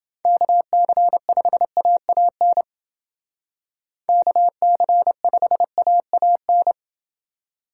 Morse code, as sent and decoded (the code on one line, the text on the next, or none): -.- -.-. ..... .- .- -. / -.- -.-. ..... .- .- -.
KC5AAN KC5AAN